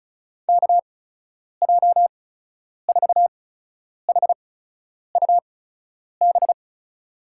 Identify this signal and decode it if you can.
Morse code, heard as a keyed tone, about 34 wpm, spaced as for 10 wpm, 700 Hz. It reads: KJ4HUB
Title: USA Call Signs - 6 Characters - 8x Character Spacing - 35wpm